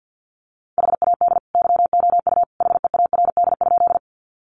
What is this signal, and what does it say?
Morse code, heard as a keyed tone, about 50 wpm, 700 Hz. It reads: HAD YOU HEARD?
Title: Sentences from Top 400 Words 50wpm